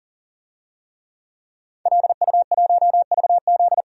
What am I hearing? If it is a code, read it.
Morse code, heard as a keyed tone, about 40 wpm, 700 Hz. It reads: LU1VZ
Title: DXCC-100 Call Signs - 5 Characters - 40wpm